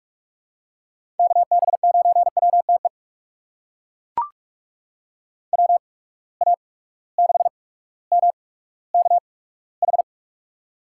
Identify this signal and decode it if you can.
Morse code, heard as a keyed tone, about 41 wpm, spaced as for 19 wpm, 700 Hz. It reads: KB9WTE WA6MKH
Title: USA Call Signs - 6 Characters - 8x Character Spacing - 45wpm